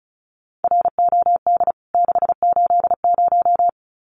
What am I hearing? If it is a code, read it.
Morse code, heard as a keyed tone, about 34 wpm, 700 Hz. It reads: ROD 680
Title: CWT Exchange 35wpm